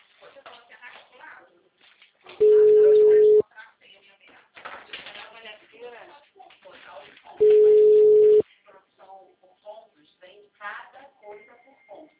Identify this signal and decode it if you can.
Ringback tone (Europe, 425 Hz), with speech.